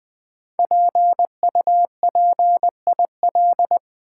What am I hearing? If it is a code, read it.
Morse code, heard as a keyed tone, about 20 wpm, 700 Hz. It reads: PUPIL
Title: 5 Letter Words - First 3 Letters Given - 20wpm